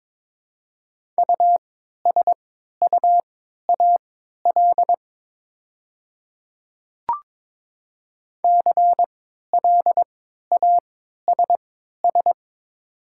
Morse code, heard as a keyed tone, about 22 wpm, spaced as for 13 wpm, 690 Hz. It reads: USUAL CLASS
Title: 3x Character Spacing - Top 500 Words - 22wpm